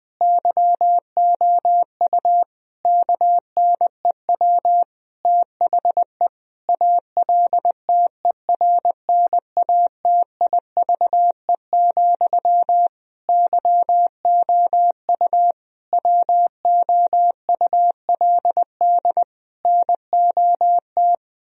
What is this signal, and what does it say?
Morse code, heard as a keyed tone, about 21 wpm, 700 Hz. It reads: YOU KNEW THE ALTERNATIVE, YOU WOULD NOT
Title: Sentences from Top 1000 Words - Review Part 4 - 20wpm